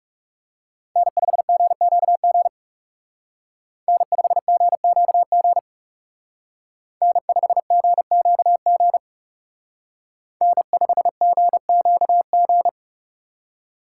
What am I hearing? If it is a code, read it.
Morse code, heard as a keyed tone, about 35 wpm, 700 Hz. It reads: N5GQG N5GQG N5GQG N5GQG
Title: Speed Racing - USA 1x3 Call Signs 30wpm